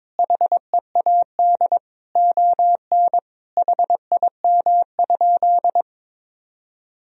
Morse code, heard as a keyed tone, about 22 wpm, 700 Hz. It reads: HEAD ON HIM?
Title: Sentences from Top 1000 Words - Review Part 4 - 22wpm